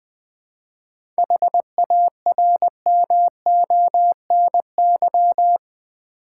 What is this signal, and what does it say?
Morse code, heard as a keyed tone, about 20 wpm, 700 Hz. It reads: HARMONY
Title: 7 Letter Words - First 3 Letters Given - 20wpm